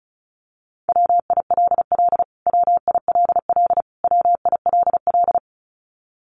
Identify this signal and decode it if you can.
Morse code, heard as a keyed tone, about 34 wpm, 700 Hz. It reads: WILL WILL WILL
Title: Top 200 Words Repeated 35wpm